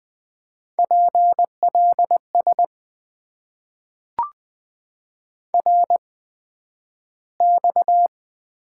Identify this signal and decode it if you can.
Morse code, heard as a keyed tone, about 20 wpm, spaced as for 11 wpm, 700 Hz. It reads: PLS RX
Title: Top 90 QSO Elements - 8x Character Spacing - 20wpm